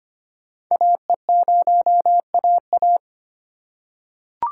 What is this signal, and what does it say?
Morse code, heard as a keyed tone, about 25 wpm, 700 Hz. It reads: AE0AA
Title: USA Call Signs - 5 Characters - 3x Character Spacing - 25wpm